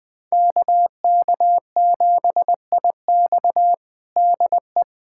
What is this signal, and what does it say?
Morse code, heard as a keyed tone, about 20 wpm, 700 Hz. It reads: KK7IX DE
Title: USA Callsign DE Callsign - 5 Characters - 20wpm